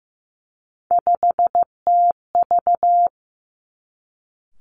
Morse code, heard as a keyed tone, about 15 wpm, 700 Hz. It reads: 5TV
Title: Three Challenging Letter-Number Combinations 15wpm